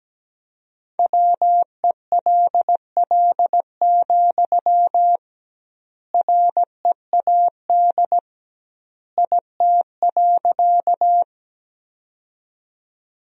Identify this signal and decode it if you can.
Morse code, heard as a keyed tone, about 17 wpm, 700 Hz. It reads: WELL, READ IT.
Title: 2x Word-Spacing - Sentences from Top 200 Words - 17wpm